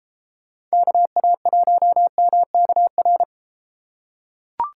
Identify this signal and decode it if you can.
Morse code, heard as a keyed tone, about 32 wpm, 700 Hz. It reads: KA1MKR